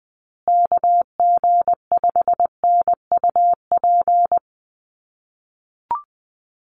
Morse code, heard as a keyed tone, about 20 wpm, 700 Hz. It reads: KG5NUP
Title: USA Call Signs Repeated 2x 20wpm